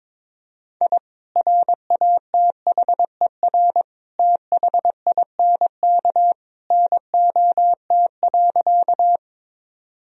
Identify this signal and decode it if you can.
Morse code, heard as a keyed tone, about 21 wpm, 700 Hz. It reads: I RATHER THINK NOT.
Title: Sentences from Top 1000 Words - Review Part 4 - 22wpm